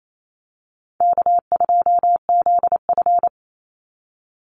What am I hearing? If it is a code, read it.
Morse code, heard as a keyed tone, about 28 wpm, 700 Hz. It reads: K2ZF
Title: Easy Call Signs - 28wpm